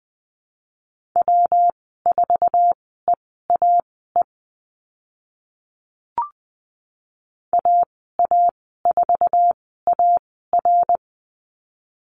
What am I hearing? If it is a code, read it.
Morse code, heard as a keyed tone, about 20 wpm, spaced as for 15 wpm, 700 Hz. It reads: W4EAE AA4AR
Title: 122 Lesson 20g - Number 3 - Call Signs 20wpm